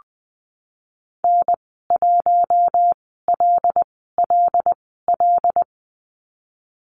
Morse code, heard as a keyed tone, about 20 wpm, spaced as for 15 wpm, 700 Hz. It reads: N1LLL
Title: 088 Lesson 15g - Number 5 - Call Signs 20wpm